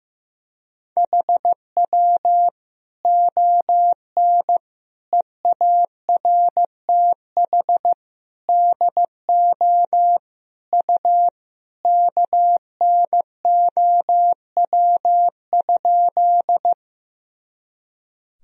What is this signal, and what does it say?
Morse code, heard as a keyed tone, about 15 wpm, 700 Hz. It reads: HW ON EARTH DO U KNOW?
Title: Sentences from Top 200 Words and Abbreviations 15wpm